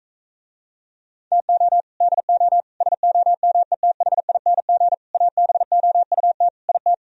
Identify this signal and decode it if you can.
Morse code, heard as a keyed tone, about 45 wpm, 700 Hz. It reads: TO DO SOMETHING ABOUT IT